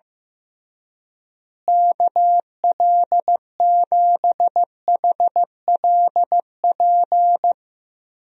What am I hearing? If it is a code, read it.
Morse code, heard as a keyed tone, about 15 wpm, 700 Hz. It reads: KL7HLP
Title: DXCC-100 Call Signs - 6 Characters - Repeated 3x - 15wpm